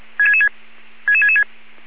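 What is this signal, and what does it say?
Five-tone selective call (ZVEI), heard as quick runs of five tones, tones of about 70 ms. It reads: 66666
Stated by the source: ZVEI five-tone selective call 77777 received by SDR.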